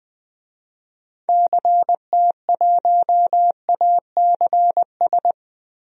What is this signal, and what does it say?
Morse code, heard as a keyed tone, about 20 wpm, 700 Hz. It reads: CT1ACS
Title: DXCC-100 Call Signs - 6 Characters - Repeated 2x - 20wpm